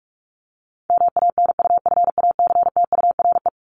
Morse code, heard as a keyed tone, about 44 wpm, 700 Hz. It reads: MANUFACTURE
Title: ING Suffix - Top 100 Words with Base Words 45wpm